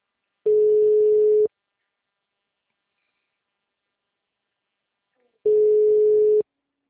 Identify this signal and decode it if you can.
Ringback tone (Europe, 425 Hz).
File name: out- -2022- - - .6561